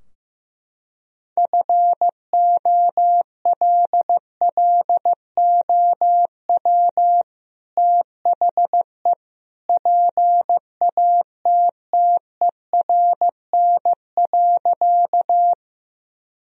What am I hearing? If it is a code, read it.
Morse code, heard as a keyed tone, about 15 wpm, 700 Hz. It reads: FOLLOW THE PATTERN.
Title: Sentences from Top 1000 Words - Review Part 2 - 15wpm